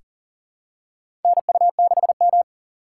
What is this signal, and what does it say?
Morse code, heard as a keyed tone, about 37 wpm, 700 Hz. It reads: NU6M
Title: USA Call Signs - 2x Character Spacing - 40wpm